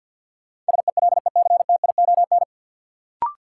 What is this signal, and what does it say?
Morse code, heard as a keyed tone, about 52 wpm, 700 Hz. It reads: SELECTION
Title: Sentences from Top 1400 Words - New Word Drill - 50wpm